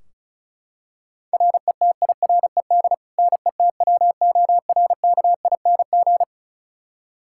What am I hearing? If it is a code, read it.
Morse code, heard as a keyed tone, about 36 wpm, 700 Hz. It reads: RETIRED NETWORKING
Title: Practice QSO - Exchange 3 - More Info - 35wpm